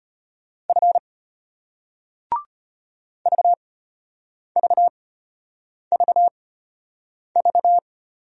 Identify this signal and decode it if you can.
Morse code, heard as a keyed tone, about 34 wpm, spaced as for 8 wpm, 700 Hz. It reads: F VVVV